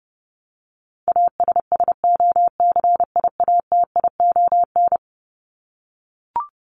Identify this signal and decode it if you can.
Morse code, heard as a keyed tone, about 30 wpm, 700 Hz. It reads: ASSOCIATION
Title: Top 1001-1250 Words 30wpm